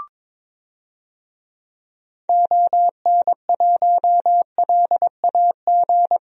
Morse code, heard as a keyed tone, about 22 wpm, 700 Hz. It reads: ON1LAG
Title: DXCC-100 Call Signs - 6 Characters - Repeated 3x - 22wpm